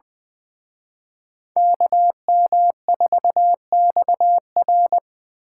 Morse code, heard as a keyed tone, about 24 wpm, 700 Hz. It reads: KM4XR
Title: Speed Racing - USA 2x2 Call Signs 20wpm